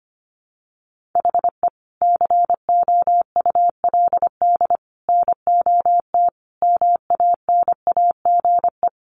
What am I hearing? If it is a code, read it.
Morse code, heard as a keyed tone, about 25 wpm, 700 Hz. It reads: HE COULD NOT MANAGE
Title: Sentences from Top 1400 Words 25wpm